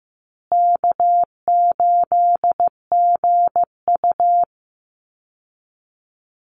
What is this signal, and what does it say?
Morse code, heard as a keyed tone, about 15 wpm, 700 Hz. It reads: K8GU
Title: Easy Call Signs Repeated 3x - 15wpm